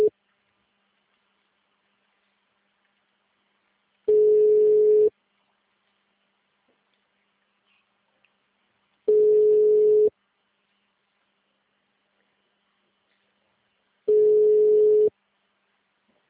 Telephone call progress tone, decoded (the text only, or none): ringback tone